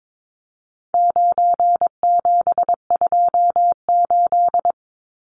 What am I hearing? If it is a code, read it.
Morse code, heard as a keyed tone, about 22 wpm, 700 Hz. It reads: 9728
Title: Numbers - 4-Digit 22wpm